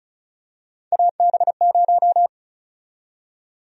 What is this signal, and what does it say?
Morse code, heard as a keyed tone, about 34 wpm, 700 Hz. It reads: AB0
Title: USA Call Sign Prefixes - 35wpm